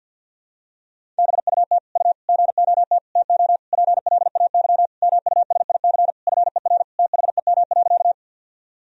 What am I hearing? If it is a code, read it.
Morse code, heard as a keyed tone, about 46 wpm, 700 Hz. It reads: BUT U GOT TO PLAY MUSIC FER THEM.
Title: Sentences from Top 300 Words and Abbreviations 50wpm